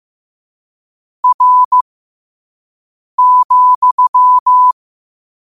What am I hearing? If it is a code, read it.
Morse code, heard as a keyed tone, about 15 wpm, spaced as for 5 wpm, 1000 Hz. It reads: R,
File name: qotdfiles / 5wpm